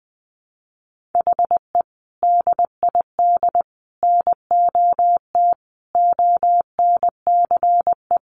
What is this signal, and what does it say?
Morse code, heard as a keyed tone, about 20 wpm, 700 Hz. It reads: HE DID NOT ONCE
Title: Sentences from Top 1500 Words 20wpm